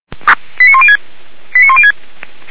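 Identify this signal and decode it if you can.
Five-tone selective call (ZVEI), heard as quick runs of five tones, tones of about 80 ms. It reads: 89106 89106
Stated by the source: ZVEI five-tone selective call received on SDR.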